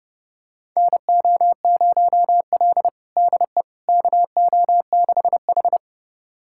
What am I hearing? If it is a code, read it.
Morse code, heard as a keyed tone, about 30 wpm, 700 Hz. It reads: NO0L DE KO6H
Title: USA Callsign DE Callsign - 4 Characters - 30wpm